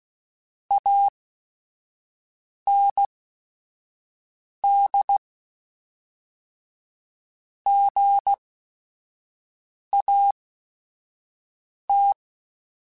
Morse code, heard as a keyed tone, about 16 wpm, spaced as for 5 wpm, 800 Hz. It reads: AND GAT
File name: Technical Subjects / Technical Subjects-mp3s / AND Gate @5